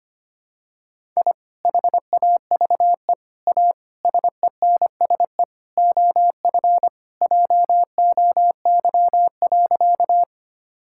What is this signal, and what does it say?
Morse code, heard as a keyed tone, about 25 wpm, 700 Hz. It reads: I HAVE A SENSE OF JOY.
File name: practice / 25wpm / Sentences with 3-Word Phrases - 25wpm